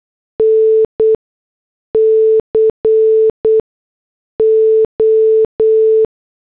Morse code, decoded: NCO